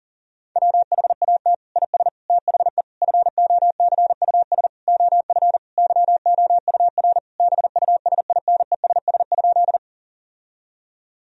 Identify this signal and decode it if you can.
Morse code, heard as a keyed tone, about 40 wpm, 700 Hz. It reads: WHAT IS THE FOCUS OF YOUR BUSINESS?
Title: Sentences from Top 1000 Words - Review Part 4 - 40wpm